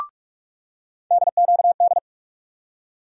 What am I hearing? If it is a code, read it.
Morse code, heard as a keyed tone, about 42 wpm, 700 Hz. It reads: DQD
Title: USA Call Sign Postfixes - 3 Letters 45wpm